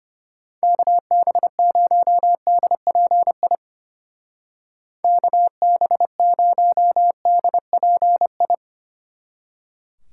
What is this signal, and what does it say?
Morse code, heard as a keyed tone, about 31 wpm, 700 Hz. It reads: KB0DPI KB0DPI
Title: Speed Racing - USA 2x3 Call Signs 25wpm